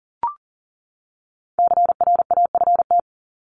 Morse code, heard as a keyed tone, about 40 wpm, 700 Hz. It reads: CRAFT
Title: Top 751-1000 Words 40wpm